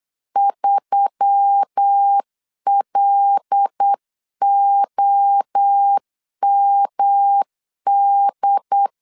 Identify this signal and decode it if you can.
Morse code, heard as a keyed tone, about 8 wpm, 800 Hz. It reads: 3LOMD
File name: LEZIONI / 7 GF9Z / 7 G 8wpm 3118